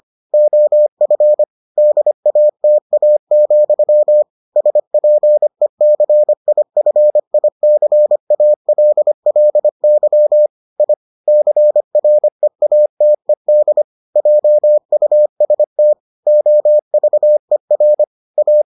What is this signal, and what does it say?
Morse code, heard as a keyed tone, about 25 wpm, 600 Hz. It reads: OF DATA, SPECIFICALLY I CREATED JUST OVER A